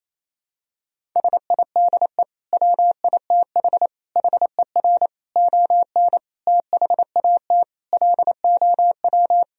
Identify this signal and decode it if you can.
Morse code, heard as a keyed tone, about 28 wpm, 700 Hz. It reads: SIDE WITH HER ON THAT LOW
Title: Sentences from Top 1000 Words - Review Part 1 - 28wpm